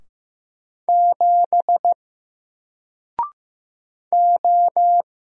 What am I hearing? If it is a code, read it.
Morse code, heard as a keyed tone, about 15 wpm, 700 Hz. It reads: 7 O
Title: Single Letter-Number Alt 15wpm